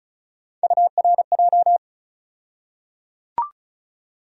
Morse code, heard as a keyed tone, about 36 wpm, 700 Hz. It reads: URJ
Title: USA Call Sign Postfixes - 3 Letters 35wpm